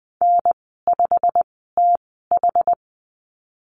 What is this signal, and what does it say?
Morse code, heard as a keyed tone, about 20 wpm, spaced as for 15 wpm, 700 Hz. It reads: N5TH
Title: 122 Lesson 20g - Number 3 - Call Signs 20wpm